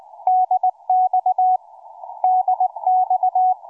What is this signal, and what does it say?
Morse code, heard as a keyed tone, about 20 wpm, 740 Hz. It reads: DX DX
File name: cw forum / 4z5lu 20m 19h25